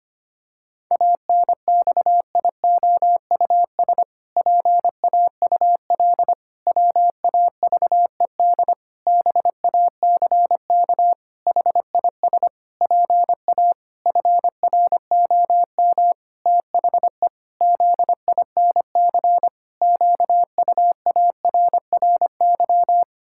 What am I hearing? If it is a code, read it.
Morse code, heard as a keyed tone, about 25 wpm, 700 Hz. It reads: ANXIOUS PAUL WAVED BACK HIS PA FROM THE ZINC QUARRY
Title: Pangram Challenge v2 - 25wpm